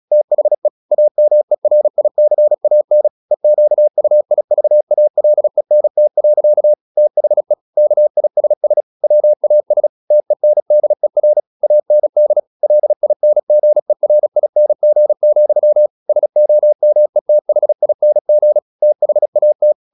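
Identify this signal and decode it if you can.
Morse code, heard as a keyed tone, about 36 wpm, 600 Hz. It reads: THE AMERICAN EQUIVALENT. THE KISS WAS TENDER AND LINGERING, SOMETHING THAT
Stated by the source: Morse practice at 36 wpm, text of Ken Follett's novel Never, from CW med Gnister.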